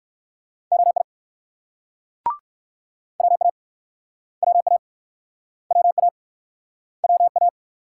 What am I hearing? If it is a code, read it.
Morse code, heard as a keyed tone, about 60 wpm, 700 Hz. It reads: OI WA WA WA WA